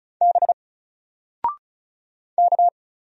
Morse code, heard as a keyed tone, about 35 wpm, 700 Hz. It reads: B K
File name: Single-Letter-Number-35wpm